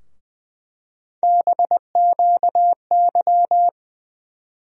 Morse code, heard as a keyed tone, about 20 wpm, 700 Hz. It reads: BQY
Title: USA Call Sign Postfixes - 3 Letters 20wpm